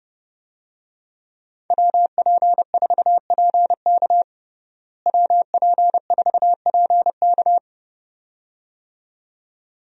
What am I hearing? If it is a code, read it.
Morse code, heard as a keyed tone, about 30 wpm, 700 Hz. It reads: WP4PK WP4PK
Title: USA Call Signs - 5 Characters - Repeated 2x - 30wpm